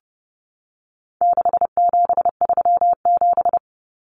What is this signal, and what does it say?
Morse code, heard as a keyed tone, about 30 wpm, 700 Hz. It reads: 6737
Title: Common Cut Numbers - 4-Digit 30wpm